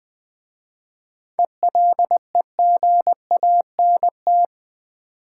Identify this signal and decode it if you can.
Morse code, heard as a keyed tone, about 20 wpm, 700 Hz. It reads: ELEGANT
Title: 7 Letter Words - First 3 Letters Given - 20wpm